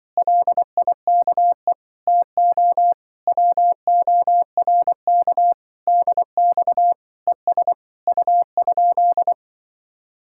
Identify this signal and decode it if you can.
Morse code, heard as a keyed tone, about 25 wpm, 700 Hz. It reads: LIKE TO WORK DX ES U?
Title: Speed-Racing Common QSO Phrases - 20wpm